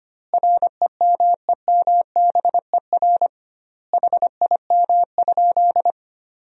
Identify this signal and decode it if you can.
Morse code, heard as a keyed tone, about 25 wpm, 700 Hz. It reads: REMEMBER HIM?
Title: 2x Word-Spacing - Sentences from Top 400 Words - 25wpm